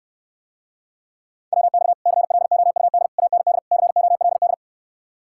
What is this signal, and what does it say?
Morse code, heard as a keyed tone, about 69 wpm, 700 Hz. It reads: WX CLOUD AND COLD